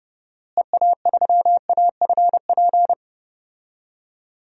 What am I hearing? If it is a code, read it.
Morse code, heard as a keyed tone, about 30 wpm, 700 Hz. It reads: EA3AFP
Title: DXCC-100 Call Signs - 6 Characters - Repeated 2x - 30wpm